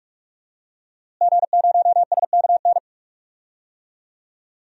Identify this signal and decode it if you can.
Morse code, heard as a keyed tone, about 43 wpm, 700 Hz. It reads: G0SKN
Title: DXCC-100 Call Signs - 5 Characters - Repeated 2x - 45wpm